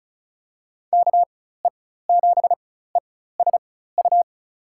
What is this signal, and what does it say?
Morse code, heard as a keyed tone, about 36 wpm, spaced as for 16 wpm, 700 Hz. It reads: KE7ESU
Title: USA Call Signs - 6 Characters - 4x Character Spacing - 35wpm